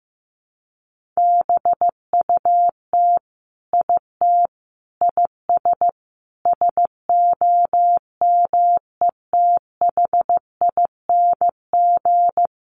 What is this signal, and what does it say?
Morse code, heard as a keyed tone, about 15 wpm, 700 Hz. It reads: BUT IT IS SOMETHING
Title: Sentences from Top 1000 Words 15wpm